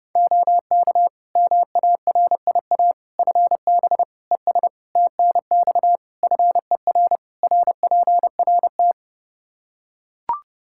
Morse code, heard as a keyed tone, about 30 wpm, 700 Hz. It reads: OK MARIA FB ES TNX FER RPRT
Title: Practice QSO - Exchange 2 - Info - 30wpm